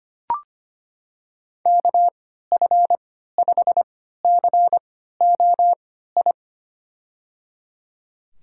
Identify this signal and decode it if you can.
Morse code, heard as a keyed tone, about 25 wpm, spaced as for 14 wpm, 700 Hz. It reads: KF5COI
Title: USA Call Signs - 6 Characters - 3x Character Spacing - 25wpm